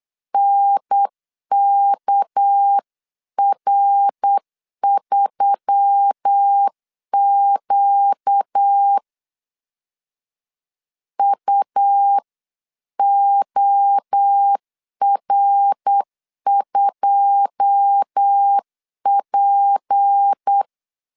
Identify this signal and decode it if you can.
Morse code, heard as a keyed tone, about 8 wpm, 800 Hz. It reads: NKR3Q UOR2P
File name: CW / DET / 1 9 1500C S A 8wpm 5637